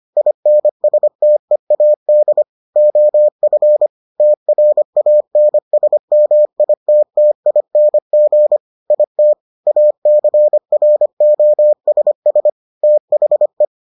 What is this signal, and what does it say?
Morse code, heard as a keyed tone, about 25 wpm, 600 Hz. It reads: INSTEAD OF TRANSMITTING IT ACROSS THE